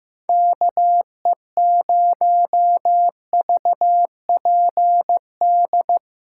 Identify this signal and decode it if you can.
Morse code, heard as a keyed tone, about 15 wpm, 700 Hz. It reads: KE0VPD
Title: USA Callsign DE Callsign - 6 Characters - 15wpm